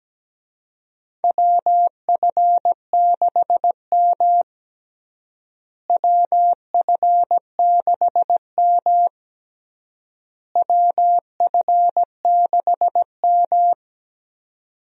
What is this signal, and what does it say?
Morse code, heard as a keyed tone, about 17 wpm, 700 Hz. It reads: WF6M WF6M WF6M
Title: USA Call Signs - 4 Characters - Repeated 3x - 17wpm